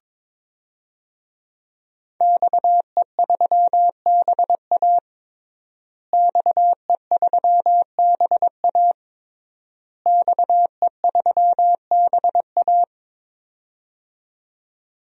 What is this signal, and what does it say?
Morse code, heard as a keyed tone, about 22 wpm, 700 Hz. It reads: XE3BA XE3BA XE3BA
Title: DXCC-100 Call Signs - 5 Characters - Repeated 3x - 22wpm